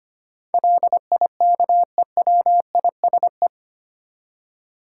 Morse code, heard as a keyed tone, about 25 wpm, 700 Hz. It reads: LIKEWISE
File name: Eight Letter Words - 25wpm